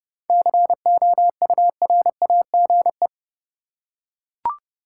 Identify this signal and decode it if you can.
Morse code, heard as a keyed tone, about 30 wpm, 700 Hz. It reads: COURAGE
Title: Seven Letter Words - 30wpm